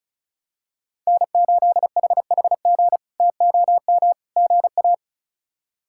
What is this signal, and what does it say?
Morse code, heard as a keyed tone, about 36 wpm, 700 Hz. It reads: N8HHG TOM GA
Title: K1USN SST QSO Practice - 35wpm